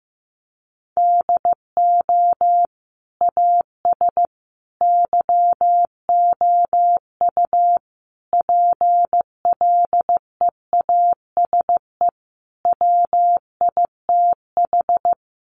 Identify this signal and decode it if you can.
Morse code, heard as a keyed tone, about 15 wpm, 700 Hz. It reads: DO AS YOU PLEASE WITH
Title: Sentences from Top 1500 Words 15wpm